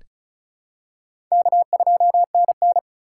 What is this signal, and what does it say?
Morse code, heard as a keyed tone, about 36 wpm, 700 Hz. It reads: K2NN
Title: USA Call Signs - 4x Character Spacing - 35wpm